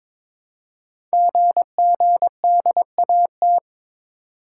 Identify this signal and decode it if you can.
Morse code, heard as a keyed tone, about 23 wpm, 700 Hz. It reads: GGDAT
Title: Code Groups - 5 Letters - 22wpm